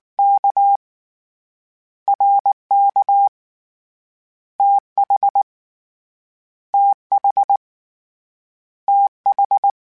Morse code, heard as a keyed tone, about 19 wpm, 800 Hz. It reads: K R K T H T H T H